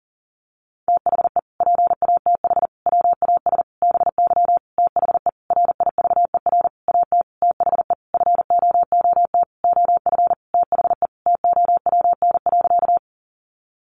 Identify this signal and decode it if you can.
Morse code, heard as a keyed tone, about 40 wpm, 700 Hz. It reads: THE PATH WAS BY THE RIVER AT THE FOOT OF THE TOWN.
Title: Sentences from Top 1400 Words 40wpm